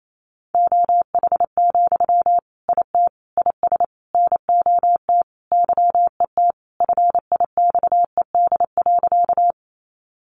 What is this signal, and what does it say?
Morse code, heard as a keyed tone, about 28 wpm, 700 Hz. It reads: OH, IT IS NOT YET FIXED.
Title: Sentences from Top 1100 Words 28wpm